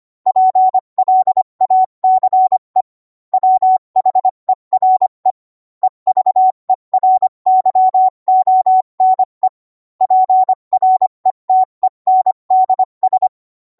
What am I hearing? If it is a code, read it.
Morse code, heard as a keyed tone, about 25 wpm, 750 Hz. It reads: PLACE WHERE EVERYONE PRETENDS